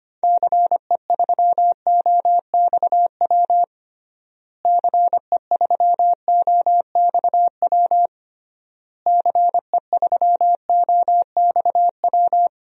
Morse code, heard as a keyed tone, about 25 wpm, 700 Hz. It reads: CE3OXW CE3OXW CE3OXW
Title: DXCC-100 Call Signs - 6 Characters - Repeated 3x - 25wpm